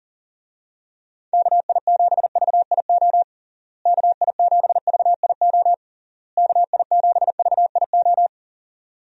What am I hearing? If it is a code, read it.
Morse code, heard as a keyed tone, about 40 wpm, 700 Hz. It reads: KI7VIO KI7VIO KI7VIO
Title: USA Call Signs - 6 Characters - Repeated 3x - 40wpm